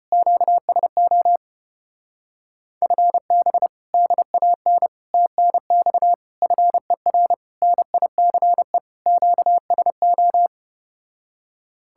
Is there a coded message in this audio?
Yes: Morse code from 0.1 to 10.5 s, heard as a keyed tone, about 33 wpm, 700 Hz. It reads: QSO FB DAN TNX FER NICE QSO